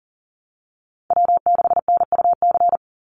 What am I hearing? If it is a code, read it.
Morse code, heard as a keyed tone, about 40 wpm, 700 Hz. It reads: W6NUC
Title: Call Signs Repeated 2x - 40wpm